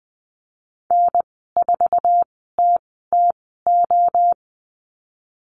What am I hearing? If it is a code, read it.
Morse code, heard as a keyed tone, about 20 wpm, spaced as for 15 wpm, 700 Hz. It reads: N4TTO
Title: 088 Lesson 15g - Number 5 - Call Signs 20wpm